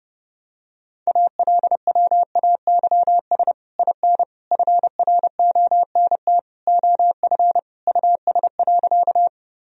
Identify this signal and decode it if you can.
Morse code, heard as a keyed tone, about 30 wpm, 700 Hz. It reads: ALWAYS IN FRONT OF US.
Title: Sentences from Top 1000 Words - Review Part 2 - 30wpm